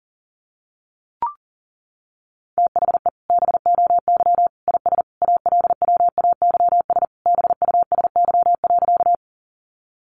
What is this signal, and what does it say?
Morse code, heard as a keyed tone, about 40 wpm, 700 Hz. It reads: THE BOY IS ALWAYS BUSY.
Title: Additional Easy Phrases 40wpm